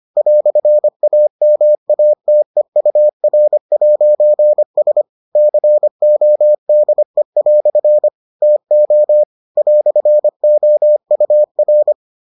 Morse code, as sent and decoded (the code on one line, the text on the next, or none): .-..-. .- -- .- - . ..- .-. .----. ... / -.-. --- -.. .-..-. / - --- / .-..-. --- ..- .-.
"AMATEUR'S CODE" TO "OUR